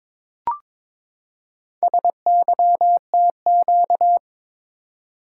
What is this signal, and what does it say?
Morse code, heard as a keyed tone, about 22 wpm, 700 Hz. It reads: SYTQ
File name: practice / 22wpm / Four Letter-Number Combinations - 22wpm